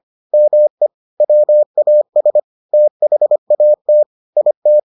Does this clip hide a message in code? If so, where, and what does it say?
Yes: Morse code from 0.3 to 4.8 s, heard as a keyed tone, about 25 wpm, 600 Hz. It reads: ME WAS THAT IT